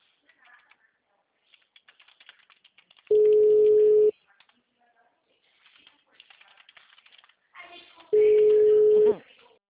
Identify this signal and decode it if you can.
Ringback tone (Europe, 425 Hz), with speech.